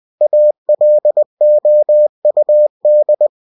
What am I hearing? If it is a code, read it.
Morse code, heard as a keyed tone, about 20 wpm, 600 Hz. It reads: ALOUD